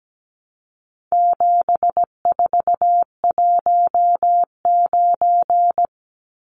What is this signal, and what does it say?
Morse code, heard as a keyed tone, about 17 wpm, 700 Hz. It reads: 7419
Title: Numbers - 4-Digit 17wpm